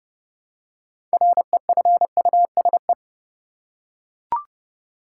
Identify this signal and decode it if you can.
Morse code, heard as a keyed tone, about 30 wpm, 700 Hz. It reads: REFUSE
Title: Six Letter Words - 30wpm